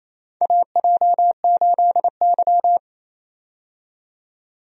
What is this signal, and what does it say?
Morse code, heard as a keyed tone, about 28 wpm, 700 Hz. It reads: AJ8Y
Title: USA Call Signs - 4 Characters - 28wpm